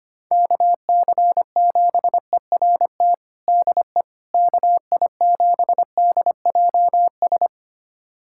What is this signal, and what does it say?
Morse code, heard as a keyed tone, about 25 wpm, 700 Hz. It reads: KC7ERT DE KI7DJS
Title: USA Callsign DE Callsign - 6 Characters - 25wpm